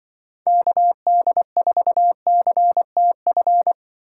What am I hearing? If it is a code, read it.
Morse code, heard as a keyed tone, about 25 wpm, 700 Hz. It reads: KD4CTF
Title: Speed Racing - USA 2x3 Call Signs 20wpm